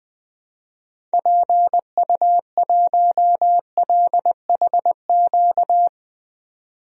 Morse code, heard as a keyed tone, about 20 wpm, 700 Hz. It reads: PU1LHQ
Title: DXCC-100 Call Signs - 6 Characters - 20wpm